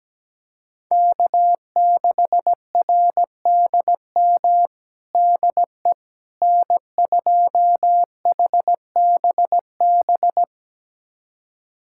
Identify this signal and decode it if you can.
Morse code, heard as a keyed tone, about 17 wpm, 700 Hz. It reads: K6RDM DE N2HBB
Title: USA Callsign DE Callsign - 5 Characters - 17wpm